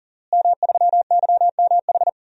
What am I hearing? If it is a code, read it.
Morse code, heard as a keyed tone, about 37 wpm, 690 Hz. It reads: M3YMH
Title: DXCC-100 Call Signs - 5 Characters - Repeated 2x - 40wpm